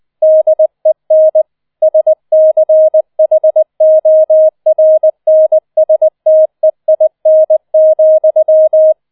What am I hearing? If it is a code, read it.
Morse code, heard as a keyed tone, about 20 wpm, 630 Hz. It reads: DEN SCHORNSTEIN,